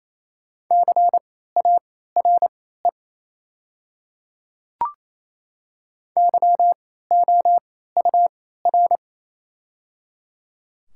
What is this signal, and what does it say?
Morse code, heard as a keyed tone, about 27 wpm, spaced as for 16 wpm, 690 Hz. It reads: CARE YOUR
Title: 3x Character Spacing - Top 300 Words - 28wpm